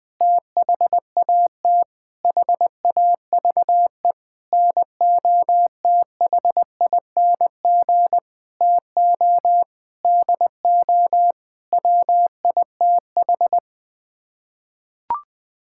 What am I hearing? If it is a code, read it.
Morse code, heard as a keyed tone, about 20 wpm, 700 Hz. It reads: THAT HAVE NOTHING TO DO WITH
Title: Sets of 8 Words - 20wpm